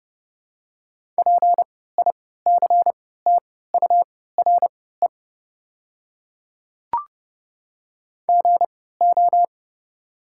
Morse code, heard as a keyed tone, about 30 wpm, spaced as for 17 wpm, 700 Hz. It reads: PICTURE GO
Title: 3x Character Spacing - Top 200 Words - 30wpm